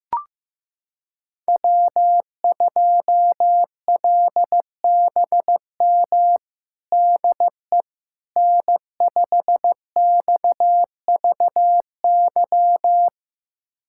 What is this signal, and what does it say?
Morse code, heard as a keyed tone, about 15 wpm, 700 Hz. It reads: W2LBM DE N5XVY
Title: USA Callsign DE Callsign 15wpm